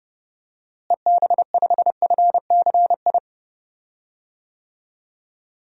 Morse code, heard as a keyed tone, about 31 wpm, 700 Hz. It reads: EB5FCI
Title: DXCC-100 Call Signs - 6 Characters - Repeated 3x - 30wpm